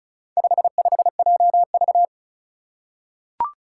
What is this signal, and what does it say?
Morse code, heard as a keyed tone, about 36 wpm, 700 Hz. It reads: 55JV
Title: Four Challenging Letter-Number Combinations 35wpm